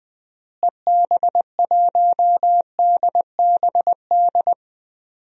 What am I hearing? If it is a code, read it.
Morse code, heard as a keyed tone, about 20 wpm, 700 Hz. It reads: EB1DBD
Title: DXCC-100 Call Signs - 6 Characters - Repeated 3x - 20wpm